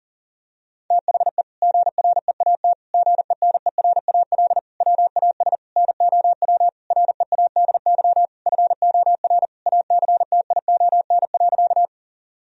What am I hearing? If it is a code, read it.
Morse code, heard as a keyed tone, about 40 wpm, 700 Hz. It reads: THE GREAT GENERAL WAS NOW READY FOR ACTION.
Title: Sentences from Top 1000 Words - Review Part 4 - 40wpm